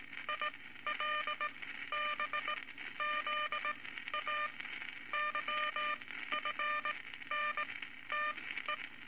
Morse code, heard as a keyed tone, about 19 wpm, spaced as for 13 wpm, 1200 Hz, with noise. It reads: ILBZA YFNTE
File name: Morse Code Records / 19 A LCM 166 Ten Words Per Minute